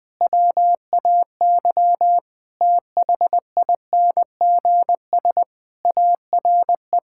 Morse code, heard as a keyed tone, about 21 wpm, 700 Hz. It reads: WAY THINGS ARE
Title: Sets of 8 Words - 20wpm